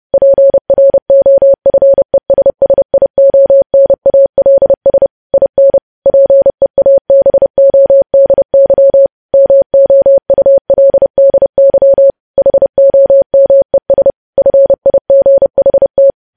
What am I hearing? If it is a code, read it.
Morse code, heard as a keyed tone, about 30 wpm, 550 Hz. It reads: PROFESSIONALS IN PEABODY MOULDY HOMES FIGHT
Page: news headlines in morse code at 30 wpm